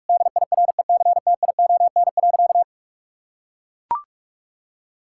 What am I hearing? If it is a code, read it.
Morse code, heard as a keyed tone, about 42 wpm, 700 Hz. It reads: DIRECTION.
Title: Sentences from Top 1000 Words - Review Part 5 - 45wpm